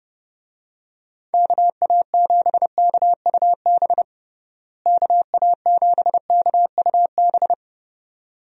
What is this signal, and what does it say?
Morse code, heard as a keyed tone, about 30 wpm, 700 Hz. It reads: KA7KUB KA7KUB
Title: USA Call Signs - 6 Characters - Repeated 2x - 30wpm